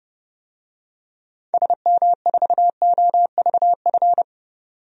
Morse code, heard as a keyed tone, about 30 wpm, 700 Hz. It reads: SM4OVF
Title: DXCC-100 Call Signs - 6 Characters - 30wpm